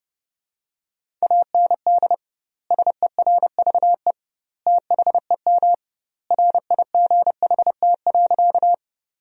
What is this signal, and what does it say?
Morse code, heard as a keyed tone, about 30 wpm, 700 Hz. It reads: AND SERVE THEM RIGHT.
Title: 2x Word-Spacing - Sentences from Top 400 Words - 30wpm